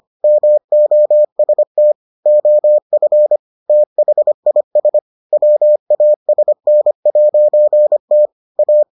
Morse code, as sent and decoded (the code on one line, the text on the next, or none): -- --- ... - / --- ..-. / - .... .. ... / .-- .- ... -. .----. - / .-
MOST OF THIS WASN'T A